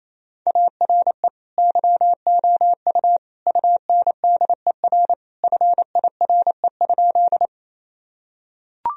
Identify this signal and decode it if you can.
Morse code, heard as a keyed tone, about 28 wpm, 700 Hz. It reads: ARE YOU UNDER FIRE?
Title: Sentences from Top 1000 Words - Review Part 1 - 28wpm